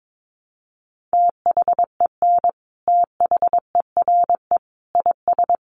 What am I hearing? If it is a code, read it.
Morse code, heard as a keyed tone, about 22 wpm, 700 Hz. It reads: THEN THERE IS
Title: Sentences from Top 100 Words 22wpm